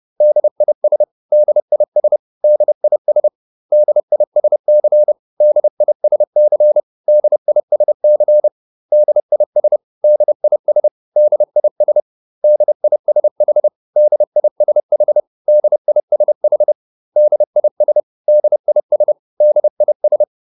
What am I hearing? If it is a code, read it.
Morse code, heard as a keyed tone, about 30 wpm, 600 Hz. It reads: DIS DIS DIS DISC DISC DISC DIS DIS DIS DISH DISH DISH DIS DIS DIS